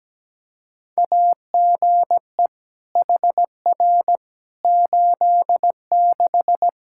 Morse code, heard as a keyed tone, about 17 wpm, 700 Hz. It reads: AGE HR 86
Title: Practice QSO - Exchange 3 - More Info - 17wpm